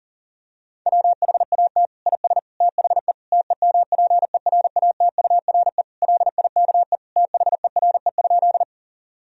Morse code, heard as a keyed tone, about 37 wpm, 700 Hz. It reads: WHAT IS THE TEMPERATURE LIKE THERE?
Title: Sentences from Top 1000 Words - Review Part 4 - 40wpm